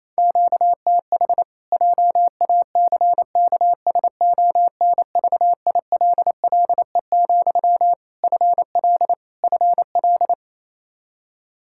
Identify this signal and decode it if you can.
Morse code, heard as a keyed tone, about 28 wpm, 700 Hz. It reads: QTH JACKSONVILLE, FL FL
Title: Practice QSO - Exchange 1 - Beginning - 28wpm